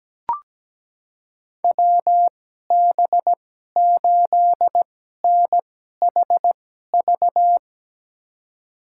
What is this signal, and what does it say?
Morse code, heard as a keyed tone, about 17 wpm, spaced as for 12 wpm, 700 Hz. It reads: WB8NHV
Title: USA Call Signs - 6 Characters - 2x Character Spacing - 17wpm